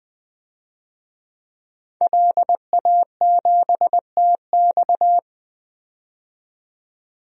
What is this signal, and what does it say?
Morse code, heard as a keyed tone, about 21 wpm, 700 Hz. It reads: LA7TX
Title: DXCC-100 Call Signs - 5 Characters - 20wpm